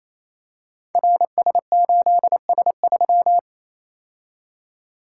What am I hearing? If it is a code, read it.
Morse code, heard as a keyed tone, about 28 wpm, 700 Hz. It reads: RS8S3
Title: Code Groups - 5 Letter:Numbers - 28wpm